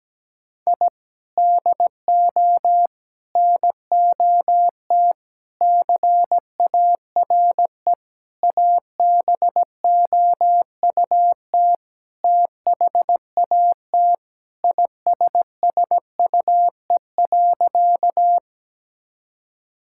Morse code, heard as a keyed tone, about 17 wpm, 700 Hz. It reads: I DO NOT CARE ABOUT THAT ISSUE.
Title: Sentences from Top 1000 Words - Review Part 6 - 17wpm